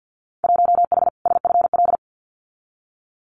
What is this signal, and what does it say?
Morse code, heard as a keyed tone, about 50 wpm, 700 Hz. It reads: 1H SFL
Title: Winter Field Day Exchanges Encore 50wpm